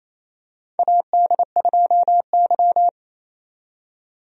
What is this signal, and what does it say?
Morse code, heard as a keyed tone, about 28 wpm, 700 Hz. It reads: AD2Y
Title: USA Call Signs - 4 Characters - 28wpm